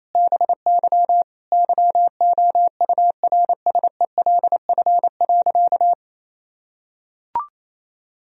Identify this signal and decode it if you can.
Morse code, heard as a keyed tone, about 27 wpm, 700 Hz. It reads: BY YOURSELF.
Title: Sentences from Top 1000 Words - Review Part 4 - 28wpm